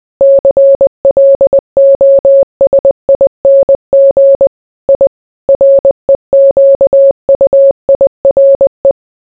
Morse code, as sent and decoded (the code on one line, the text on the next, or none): -.-. .-.. --- ... .. -. --. / .. / .-. --.- ..- .. .-. .
CLOSING I REQUIRE